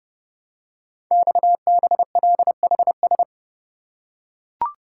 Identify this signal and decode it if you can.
Morse code, heard as a keyed tone, about 30 wpm, 700 Hz. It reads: XBLHS
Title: Code Groups - 5 Letters - 30wpm